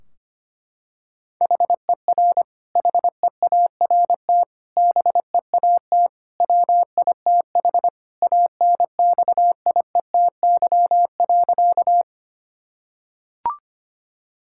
Morse code, heard as a keyed tone, about 25 wpm, 700 Hz. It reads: HER HEART BEAT WITH ANXIETY.
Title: Sentences from Top 1600 Words 25wpm